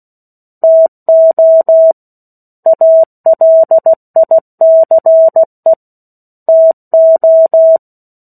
Morse code, heard as a keyed tone, about 16 wpm, 650 Hz. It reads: TO ALICE TO
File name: 03-Chapter-03 16 WPM